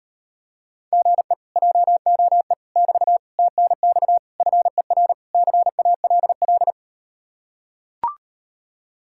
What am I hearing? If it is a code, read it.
Morse code, heard as a keyed tone, about 38 wpm, 700 Hz. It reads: GE JOE <BT> TNX FER CALL